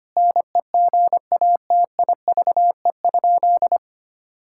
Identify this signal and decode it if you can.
Morse code, heard as a keyed tone, about 25 wpm, 700 Hz. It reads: NEGATIVE?
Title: Sentences from Top 1000 Words - Review Part 6 - 25wpm